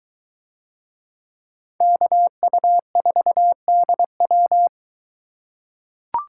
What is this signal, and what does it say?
Morse code, heard as a keyed tone, about 23 wpm, 700 Hz. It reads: KU4DW